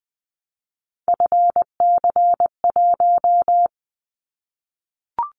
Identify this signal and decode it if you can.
Morse code, heard as a keyed tone, about 20 wpm, 700 Hz. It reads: FC1
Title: Callsign Prefixes for Top 25 DXCC - Repeated 3x - 20wpm